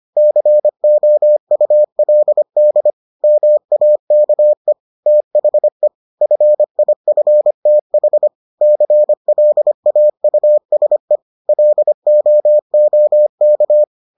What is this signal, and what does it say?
Morse code, heard as a keyed tone, about 25 wpm, 600 Hz. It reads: COULD MAKE THE FIFTH CLAUSE LOOK